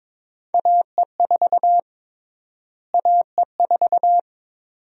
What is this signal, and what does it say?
Morse code, heard as a keyed tone, about 22 wpm, 700 Hz. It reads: AE4 AE4
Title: USA Call Sign Prefixes - Repeated 2x - 22wpm